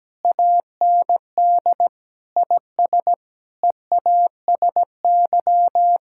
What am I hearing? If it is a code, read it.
Morse code, heard as a keyed tone, about 17 wpm, 700 Hz. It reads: AND IS EASY
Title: Sentences from Top 1000 Words - Review Part 6 - 17wpm